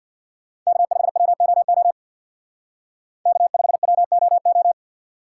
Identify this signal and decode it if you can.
Morse code, heard as a keyed tone, about 64 wpm, 700 Hz. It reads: K5WOO K5WOO